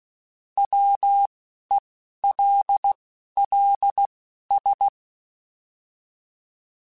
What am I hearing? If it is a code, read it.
Morse code, heard as a keyed tone, about 16 wpm, spaced as for 11 wpm, 800 Hz. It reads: WELLS